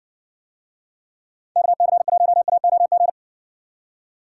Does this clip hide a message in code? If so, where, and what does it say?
Yes: Morse code from 1.6 to 3.1 s, heard as a keyed tone, about 59 wpm, 700 Hz. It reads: KC1AOG